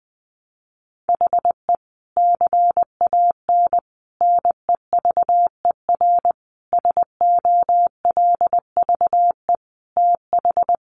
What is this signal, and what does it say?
Morse code, heard as a keyed tone, about 20 wpm, 700 Hz. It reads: HE CAN NEVER SOLVE TH
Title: Sentences from Top 1300 Words 20wpm